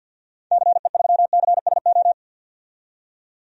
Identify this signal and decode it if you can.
Morse code, heard as a keyed tone, about 52 wpm, 700 Hz. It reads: CE3CSO
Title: DXCC-100 Call Signs - 6 Characters - 50wpm